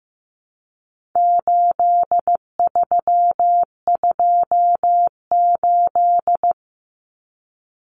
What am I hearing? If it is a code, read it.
Morse code, heard as a keyed tone, about 15 wpm, 700 Hz. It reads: 8328
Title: Numbers - 4-Digit 15wpm